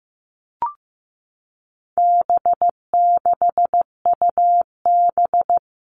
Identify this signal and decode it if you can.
Morse code, heard as a keyed tone, about 15 wpm, 700 Hz. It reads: B6UB
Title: All Cut Numbers - 4-Digit 15wpm